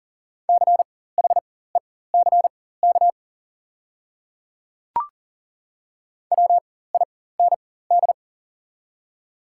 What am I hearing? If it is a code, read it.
Morse code, heard as a keyed tone, about 40 wpm, spaced as for 19 wpm, 700 Hz. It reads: CHECK WIND
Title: 4x Character Spacing - Top 500 Words - 40wpm